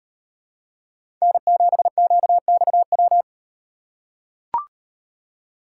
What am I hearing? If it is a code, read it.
Morse code, heard as a keyed tone, about 38 wpm, 700 Hz. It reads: N7QXW